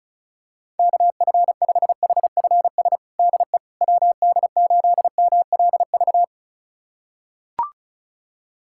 Morse code, heard as a keyed tone, about 36 wpm, 700 Hz. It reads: KF5HFS DE WD8MLV
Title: USA Callsign DE Callsign - 6 Characters - 35wpm